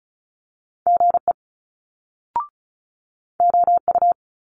Morse code, heard as a keyed tone, about 36 wpm, 700 Hz. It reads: GE OU
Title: Multi-letter Phonemes Quick Fire 35wpm